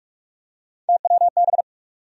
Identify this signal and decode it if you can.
Morse code, heard as a keyed tone, about 43 wpm, 700 Hz. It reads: TWB
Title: USA Call Sign Postfixes - 3 Letters 45wpm